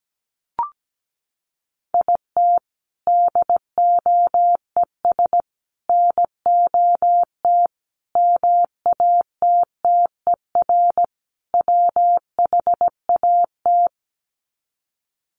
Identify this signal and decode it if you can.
Morse code, heard as a keyed tone, about 17 wpm, 700 Hz. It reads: IT DOES NOT MATTER WHAT
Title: Sets of 5 Words Encore 17wpm